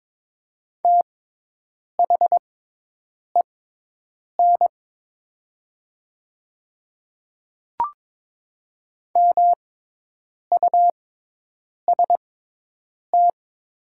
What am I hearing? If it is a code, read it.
Morse code, heard as a keyed tone, about 22 wpm, spaced as for 8 wpm, 700 Hz. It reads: THEN MUST
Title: 6x Character Spacing - Top 200 Words - 22wpm